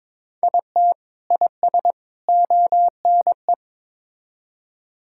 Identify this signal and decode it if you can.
Morse code, heard as a keyed tone, about 22 wpm, 700 Hz. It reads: IT IS ONE
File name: practice / 22wpm / Build A Phrase - 5-Word Phrases - 22wpm